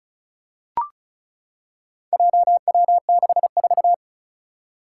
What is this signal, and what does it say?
Morse code, heard as a keyed tone, about 36 wpm, 700 Hz. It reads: JW64
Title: Four Challenging Letter-Number Combinations 35wpm